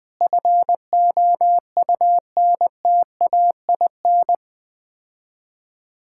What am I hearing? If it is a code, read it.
Morse code, heard as a keyed tone, about 20 wpm, 690 Hz. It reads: FOUNTAIN
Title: Difficult Words - Speed-Racing with Hints - 20wpm